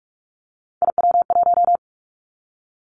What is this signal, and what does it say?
Morse code, heard as a keyed tone, about 46 wpm, 700 Hz. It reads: IW1
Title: Callsign Prefixes for Top 25 DXCC - 45wpm